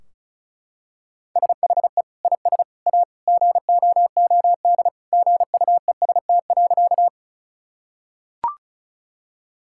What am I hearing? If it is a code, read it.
Morse code, heard as a keyed tone, about 36 wpm, 700 Hz. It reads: SHE IS A GOOD GUEST.
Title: Sentences from Top 1600 Words 35wpm